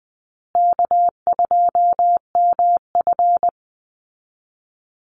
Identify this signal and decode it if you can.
Morse code, heard as a keyed tone, about 20 wpm, 700 Hz. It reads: K2MF
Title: Easy Call Signs Repeated 3x - 20wpm